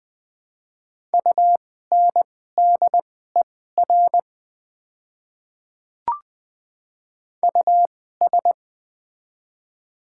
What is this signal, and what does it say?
Morse code, heard as a keyed tone, about 20 wpm, spaced as for 14 wpm, 700 Hz. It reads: UNDER US
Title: 2x Character Spacing - Top 200 Words - 20wpm